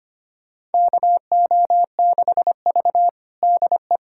Morse code, heard as a keyed tone, about 25 wpm, 700 Hz. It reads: KO6V DE
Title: USA Callsign DE Callsign - 4 Characters - 25wpm